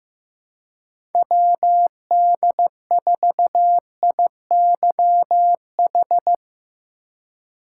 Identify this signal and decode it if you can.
Morse code, heard as a keyed tone, about 15 wpm, 700 Hz. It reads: WD4IYH
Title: USA Call Signs - 6 Characters - 8x Character Spacing - 15wpm